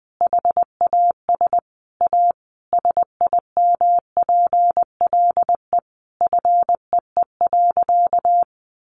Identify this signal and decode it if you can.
Morse code, heard as a keyed tone, about 19 wpm, 700 Hz. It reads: HAS A SIMPLE FEE.
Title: Sentences from Top 1500 Words 20wpm